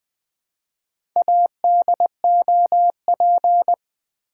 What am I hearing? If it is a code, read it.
Morse code, heard as a keyed tone, about 20 wpm, 700 Hz. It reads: ADOP
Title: Build A Word - 4 Letters At A Time - 8-Letter Words - 20wpm